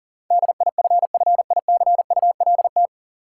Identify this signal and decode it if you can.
Morse code, heard as a keyed tone, about 37 wpm, 700 Hz. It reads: DIFFICULT
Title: Sets of 4 Words - New Word Drill - 40wpm